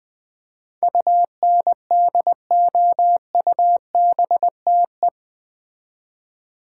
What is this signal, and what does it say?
Morse code, heard as a keyed tone, about 20 wpm, 700 Hz. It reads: UNDOUBTE
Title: Build A Word - 4 Letters At A Time - 11-Letter Words - 20wpm